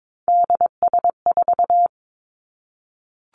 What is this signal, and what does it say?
Morse code, heard as a keyed tone, about 22 wpm, 700 Hz. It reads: DS4
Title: Three Challenging Letter-Number Combinations 22wpm